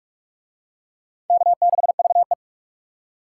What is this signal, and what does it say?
Morse code, heard as a keyed tone, about 40 wpm, 700 Hz. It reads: K6VE